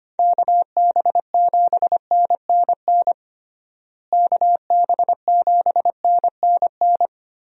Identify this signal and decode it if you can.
Morse code, heard as a keyed tone, about 25 wpm, 700 Hz. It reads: KB7NNN KB7NNN
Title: USA Call Signs - 6 Characters - Repeated 2x - 25wpm